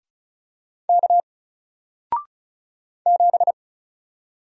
Morse code, heard as a keyed tone, about 36 wpm, 700 Hz. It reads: K 7
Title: Single Character (Letter:Number:Punctuation:Prosign) Alt 35wpm